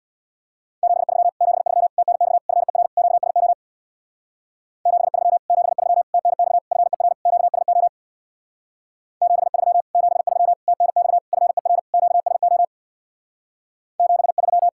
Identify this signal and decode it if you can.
Morse code, heard as a keyed tone, about 66 wpm, 700 Hz. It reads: 73 73 TNX FER QSO 73 73 TNX FER QSO 73 73 TNX FER QSO 73